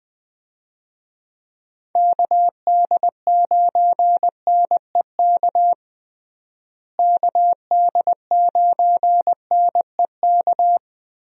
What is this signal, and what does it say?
Morse code, heard as a keyed tone, about 19 wpm, 700 Hz. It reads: KD9NEK KD9NEK
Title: USA Call Signs - 6 Characters - Repeated 3x - 20wpm